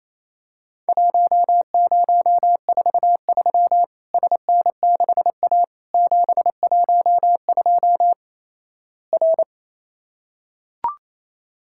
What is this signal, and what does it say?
Morse code, heard as a keyed tone, about 28 wpm, 700 Hz. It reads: 1043 SN6A 712 R
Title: Worked All Europe DX Contest - QTC Exchange - 28wpm